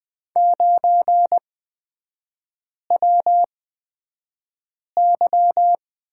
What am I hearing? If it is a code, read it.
Morse code, heard as a keyed tone, about 20 wpm, spaced as for 5 wpm, 700 Hz. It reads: 9WY
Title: 224 Lesson 36e - BT - Three Characters 20wpm